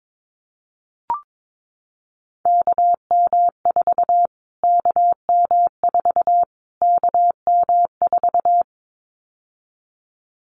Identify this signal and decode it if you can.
Morse code, heard as a keyed tone, about 22 wpm, 700 Hz. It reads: KM4 KM4 KM4
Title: Callsign Prefixes for Top 25 DXCC - Repeated 3x - 22wpm